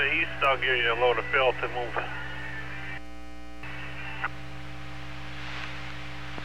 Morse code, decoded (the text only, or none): none